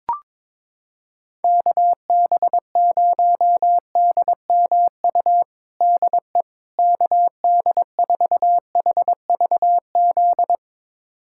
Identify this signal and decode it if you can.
Morse code, heard as a keyed tone, about 22 wpm, 700 Hz. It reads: KB0DMU DE KD4HVZ
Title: USA Callsign DE Callsign 22wpm